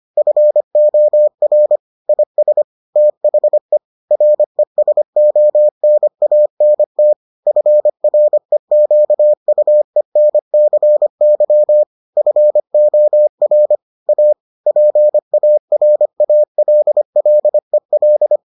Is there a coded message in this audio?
Yes: Morse code from 0.2 to 18.4 s, heard as a keyed tone, about 25 wpm, 600 Hz. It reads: FOR IS THE RESONANT FREQUENCY FOR A PARALLEL